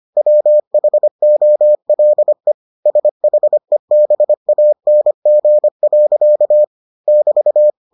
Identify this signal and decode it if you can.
Morse code, heard as a keyed tone, about 25 wpm, 600 Hz. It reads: WHOLE SHEBANG. <BT>